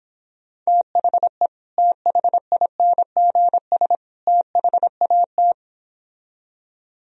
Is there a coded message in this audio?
Yes: Morse code from 0.7 to 5.5 s, heard as a keyed tone, about 26 wpm, 700 Hz. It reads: THE THINGS THAT